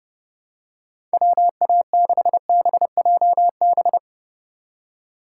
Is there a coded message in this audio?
Yes: Morse code from 1.1 to 4.0 s, heard as a keyed tone, about 31 wpm, 700 Hz. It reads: WA6BJB